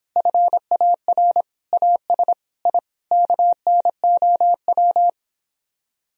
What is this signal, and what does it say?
Morse code, heard as a keyed tone, about 26 wpm, 700 Hz. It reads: FAR AS I KNOW